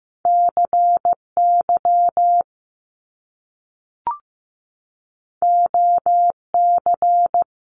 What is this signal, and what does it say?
Morse code, heard as a keyed tone, about 15 wpm, 700 Hz. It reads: CY OC
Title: Top Two Letter Combinations in Words 15wpm